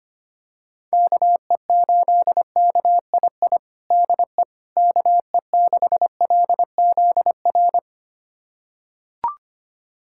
Morse code, heard as a keyed tone, about 25 wpm, 700 Hz. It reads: KE8KII DE KE6LZR
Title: USA Callsign DE Callsign - 6 Characters - 25wpm